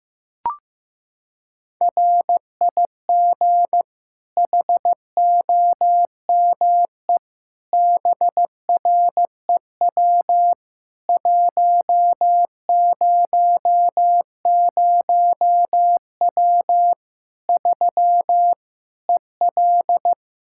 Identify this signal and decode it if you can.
Morse code, heard as a keyed tone, about 15 wpm, 700 Hz. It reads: RIG HOME BREW 100W 3 EL
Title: K1USN SST QSO Practice 15wpm